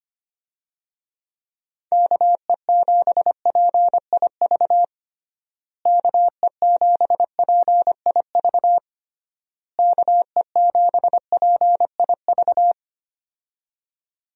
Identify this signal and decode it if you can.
Morse code, heard as a keyed tone, about 25 wpm, 700 Hz. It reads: KE7PIV KE7PIV KE7PIV
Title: USA Call Signs - 6 Characters - Repeated 3x - 25wpm